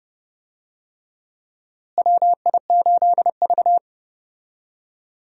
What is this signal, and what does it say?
Morse code, heard as a keyed tone, about 30 wpm, 700 Hz. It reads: WI8V
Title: USA Call Signs - 4 Characters - Repeated 3x - 30wpm